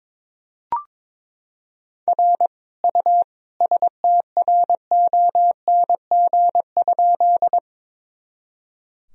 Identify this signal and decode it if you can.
Morse code, heard as a keyed tone, about 22 wpm, 700 Hz. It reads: R U STRONG?
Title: Sentences from Top 500 Words and Abbreviations 22wpm